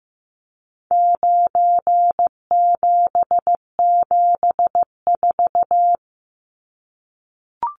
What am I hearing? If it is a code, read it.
Morse code, heard as a keyed tone, about 15 wpm, 700 Hz. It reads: 9774
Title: Numbers - 4-Digit 15wpm